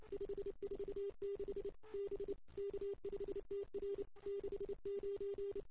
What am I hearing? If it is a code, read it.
Morse code, heard as a keyed tone, about 23 wpm, 410 Hz. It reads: 546 B K5TR 69
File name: CW / 2008-ss-cw-k5tr-op-n5rz -02:00